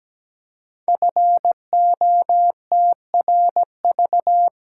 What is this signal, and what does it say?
Morse code, heard as a keyed tone, about 17 wpm, 700 Hz. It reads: FOTRV
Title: Code Groups - 5 Letters - 17wpm